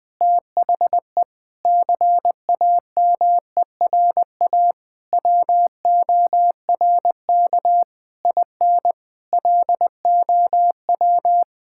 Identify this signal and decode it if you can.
Morse code, heard as a keyed tone, about 20 wpm, 700 Hz. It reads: THE CAMERA WORK IN LOW
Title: Sentences from Top 1000 Words - Review Part 5 - 20wpm